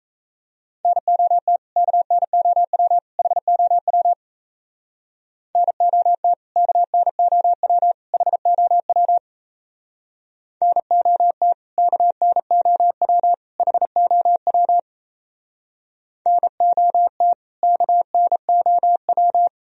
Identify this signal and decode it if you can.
Morse code, heard as a keyed tone, about 36 wpm, 700 Hz. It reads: NOT KNOW HOW NOT KNOW HOW NOT KNOW HOW NOT KNOW